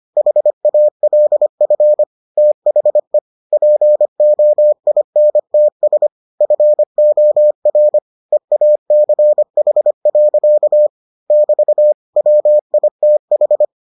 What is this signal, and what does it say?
Morse code, heard as a keyed tone, about 25 wpm, 600 Hz. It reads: HALF THE POINTS FOR EACH. <BT> WITH